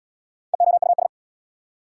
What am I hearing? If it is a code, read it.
Morse code, heard as a keyed tone, about 69 wpm, 690 Hz. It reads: EOUS